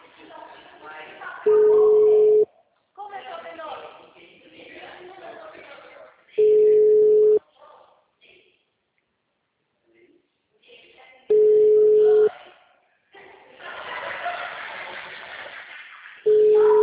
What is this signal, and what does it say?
Ringback tone (Europe, 425 Hz), with speech.
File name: out- -2011- - - .108